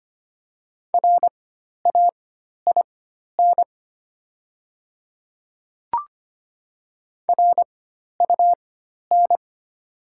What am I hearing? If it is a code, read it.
Morse code, heard as a keyed tone, about 25 wpm, spaced as for 12 wpm, 700 Hz. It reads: RAIN RUN
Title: 4x Character Spacing - Top 300 Words - 25wpm